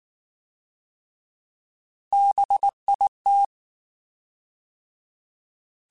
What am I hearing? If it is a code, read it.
Morse code, heard as a keyed tone, about 19 wpm, 800 Hz. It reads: BIT
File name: Alphabetical / Alphabetical mp3s / Bs @18